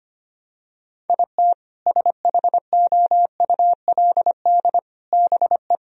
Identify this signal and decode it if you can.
Morse code, heard as a keyed tone, about 25 wpm, 700 Hz. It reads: IT SHOULD BE
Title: Sentences from Top 1000 Words - Review Part 2 - 25wpm